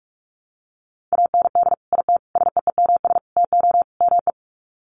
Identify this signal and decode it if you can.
Morse code, heard as a keyed tone, about 44 wpm, 700 Hz. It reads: AND IT SEEMS TO ME
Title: Sets of 5 Words Encore2 45wpm